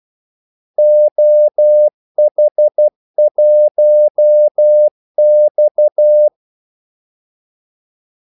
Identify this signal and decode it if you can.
Morse code, heard as a keyed tone, about 12 wpm, 600 Hz. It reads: OH1X
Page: 2 ekstra ordmellemrum mellem kaldesignalerne